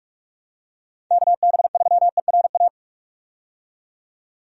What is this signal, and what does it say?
Morse code, heard as a keyed tone, about 41 wpm, 700 Hz. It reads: KB3ERA
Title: USA Call Signs - 6 Characters - 45wpm